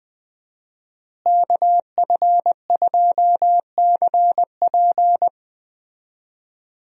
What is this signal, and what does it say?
Morse code, heard as a keyed tone, about 21 wpm, 700 Hz. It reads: KF2CP
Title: USA Call Signs - 5 Characters - Repeated 3x - 20wpm